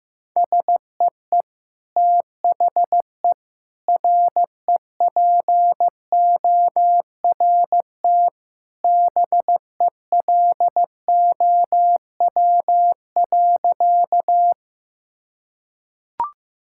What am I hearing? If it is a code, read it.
Morse code, heard as a keyed tone, about 16 wpm, 700 Hz. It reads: SEE THE REPORT BELOW.
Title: Sentences from Top 1000 Words - Review Part 6 - 15wpm